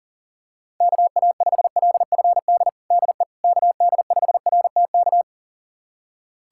Morse code, heard as a keyed tone, about 40 wpm, 700 Hz. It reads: KA5LFD DE KD5RTK
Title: USA Callsign DE Callsign - 6 Characters - 40wpm